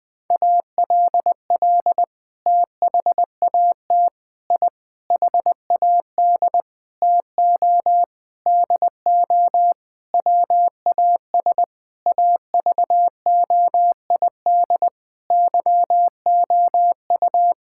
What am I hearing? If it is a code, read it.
Morse code, heard as a keyed tone, about 20 wpm, 700 Hz. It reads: ALL THAT I HAD TO DO WAS AVOID YOU
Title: Sentences from Top 1000 Words - Review Part 6 - 20wpm